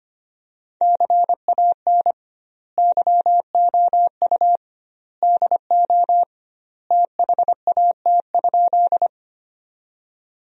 Morse code, heard as a keyed tone, about 25 wpm, 700 Hz. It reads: CAN YOU DO THAT?
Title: 2x Word-Spacing - Sentences from Top 100 Words - 25wpm